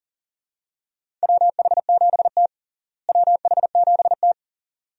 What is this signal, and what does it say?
Morse code, heard as a keyed tone, about 40 wpm, 700 Hz. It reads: WH7T WH7T
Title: USA Call Signs - 4 Characters - Repeated 2x - 40wpm